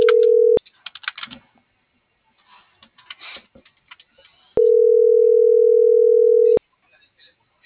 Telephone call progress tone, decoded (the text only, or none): ringback tone